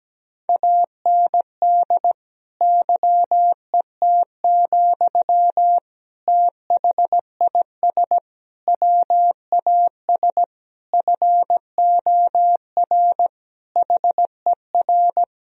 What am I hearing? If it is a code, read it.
Morse code, heard as a keyed tone, about 17 wpm, 700 Hz. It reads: AND YET, THIS WAS FOR HER
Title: Sentences from Top 1000 Words - Review Part 3 - 17wpm